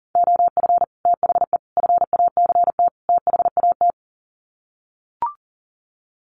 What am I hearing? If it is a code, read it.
Morse code, heard as a keyed tone, about 40 wpm, 700 Hz. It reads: OF THE FACT THAT
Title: Sets of 4 Words 40wpm